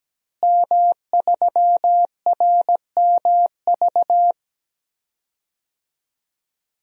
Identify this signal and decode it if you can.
Morse code, heard as a keyed tone, about 17 wpm, 700 Hz. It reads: M3RMV
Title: DXCC-100 Call Signs - 5 Characters - Repeated 2x - 17wpm